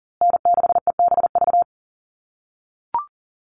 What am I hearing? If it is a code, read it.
Morse code, heard as a keyed tone, about 40 wpm, 700 Hz. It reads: N6EBV
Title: USA Call Signs Repeated 3x 40wpm